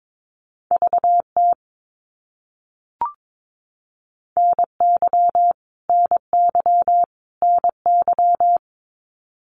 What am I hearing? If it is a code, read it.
Morse code, heard as a keyed tone, about 22 wpm, 700 Hz. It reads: VT NY NY NY
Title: US States Repeated 3x 22wpm